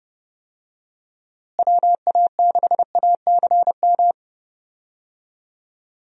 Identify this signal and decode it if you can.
Morse code, heard as a keyed tone, about 32 wpm, 700 Hz. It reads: WA6ACM
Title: USA Call Signs - 6 Characters - Repeated 3x - 30wpm